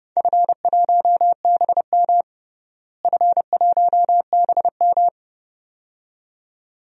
Morse code, heard as a keyed tone, about 32 wpm, 700 Hz. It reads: F1BM F1BM
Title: DXCC-100 Call Signs - 4 Characters - Repeated 2x - 30wpm